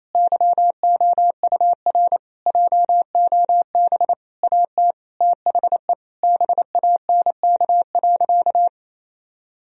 Morse code, heard as a keyed tone, about 28 wpm, 700 Hz. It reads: YOUR JOB AT THE BANK.
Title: Sentences from Top 1000 Words - Review Part 4 - 28wpm